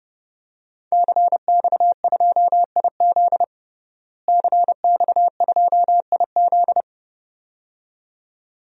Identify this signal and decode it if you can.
Morse code, heard as a keyed tone, about 30 wpm, 700 Hz. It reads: CX2IZ CX2IZ
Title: DXCC-100 Call Signs - 5 Characters - Repeated 2x - 30wpm